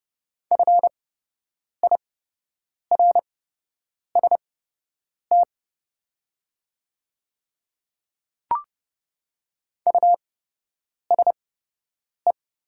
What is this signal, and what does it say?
Morse code, heard as a keyed tone, about 29 wpm, spaced as for 8 wpm, 700 Hz. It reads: FIRST USE
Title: 8x Character Spacing - Top 100 Words - 30wpm